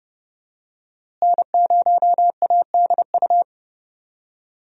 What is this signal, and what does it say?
Morse code, heard as a keyed tone, about 30 wpm, 700 Hz. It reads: N0ADU
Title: USA Call Signs - 5 Characters - Repeated 2x - 30wpm